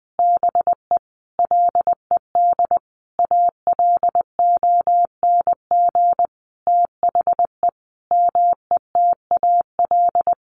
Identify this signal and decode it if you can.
Morse code, heard as a keyed tone, about 20 wpm, 700 Hz. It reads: BE LED ALONG THE METAL
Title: A Sound Of Thunder 20wpm